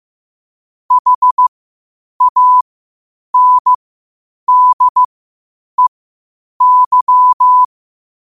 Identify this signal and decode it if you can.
Morse code, heard as a keyed tone, about 15 wpm, spaced as for 8 wpm, 1000 Hz. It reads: HANDEY
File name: qotdfiles / 8wpm